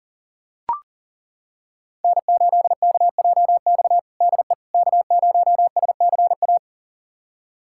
Morse code, heard as a keyed tone, about 37 wpm, 700 Hz. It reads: N8KJX DE K0SCA
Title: USA Callsign DE Callsign - 5 Characters - 40wpm